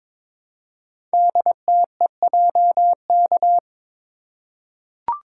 Morse code, heard as a keyed tone, about 22 wpm, 700 Hz. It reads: DTEJK
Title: Code Groups - 5 Letters - 22wpm